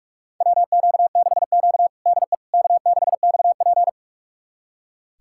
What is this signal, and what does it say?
Morse code, heard as a keyed tone, about 46 wpm, 700 Hz. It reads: WQ6Q DE K6XP
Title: USA Callsign DE Callsign - 4 Characters - 45wpm